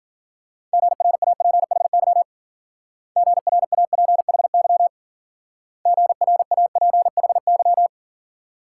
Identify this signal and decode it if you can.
Morse code, heard as a keyed tone, about 49 wpm, 700 Hz. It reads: GRAPHY GRAPHY GRAPHY
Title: Speed-Racing Multi-letter Phonemes - 40wpm